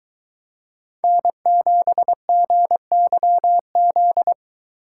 Morse code, heard as a keyed tone, about 23 wpm, 700 Hz. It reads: N7GYZ